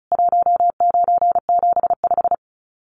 Morse code, heard as a keyed tone, about 34 wpm, 700 Hz. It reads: 1975
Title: Common QSO Phrases 35wpm